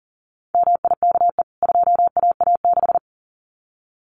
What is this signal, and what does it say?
Morse code, heard as a keyed tone, about 38 wpm, 700 Hz. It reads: MIKE 2AA6
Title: CWT Exchange 40wpm